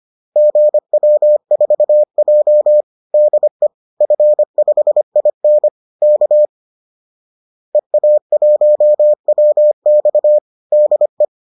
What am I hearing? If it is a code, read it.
Morse code, heard as a keyed tone, about 25 wpm, 600 Hz. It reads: GW4J DE F5IN K EA1WX DE